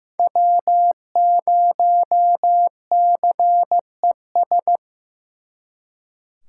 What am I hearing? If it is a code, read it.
Morse code, heard as a keyed tone, about 18 wpm, 700 Hz. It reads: W0CES
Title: Speed Racing - USA 1x3 Call Signs 15wpm